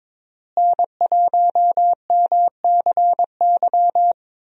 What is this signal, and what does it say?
Morse code, heard as a keyed tone, about 22 wpm, 700 Hz. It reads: N1MCY
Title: USA Call Signs - 5 Characters - Repeated 3x - 22wpm